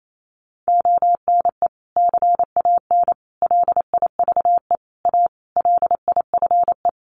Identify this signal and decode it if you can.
Morse code, heard as a keyed tone, about 28 wpm, 700 Hz. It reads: ONE CAN LIVE A LIFE
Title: Sentences from Top 1400 Words 28wpm